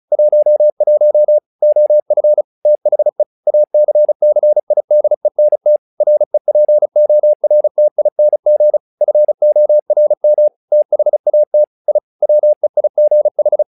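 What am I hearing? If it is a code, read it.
Morse code, heard as a keyed tone, about 36 wpm, 600 Hz. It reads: 11 OF THE ACCIDENT REPORTING FORM THAT I WEIGH
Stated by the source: Morse practice text "Tower Trouble" sent at 35 WPM, from the SKCC - Straight Key Century Club learning center audio files.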